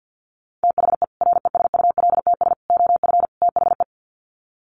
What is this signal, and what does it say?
Morse code, heard as a keyed tone, about 50 wpm, 700 Hz. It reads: THE RESULTS OF THE
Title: Sets of 4 Words 50wpm